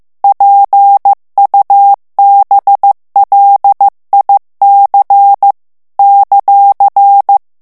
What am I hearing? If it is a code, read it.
Morse code, heard as a keyed tone, about 15 wpm, 800 Hz. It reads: PUBLIC ;
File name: FRANCAIS TEXTE CLAIR 2 VITESSE 14 MOTS MINUTES